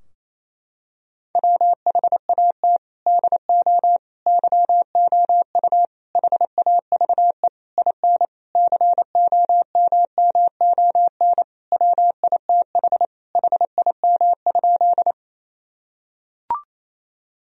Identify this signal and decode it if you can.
Morse code, heard as a keyed tone, about 28 wpm, 700 Hz. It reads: WHAT DO YOU HAVE IN COMMON WITH HIM?
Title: Sentences from Top 1000 Words - Review Part 3 - 28wpm